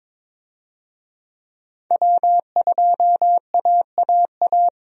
Morse code, heard as a keyed tone, about 22 wpm, 700 Hz. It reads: W2AAA
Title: USA Call Signs - 5 Characters - 22wpm